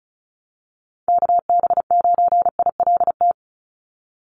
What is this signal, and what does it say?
Morse code, heard as a keyed tone, about 34 wpm, 700 Hz. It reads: KB9ILT
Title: NA SOTA Activator Callsigns 35wpm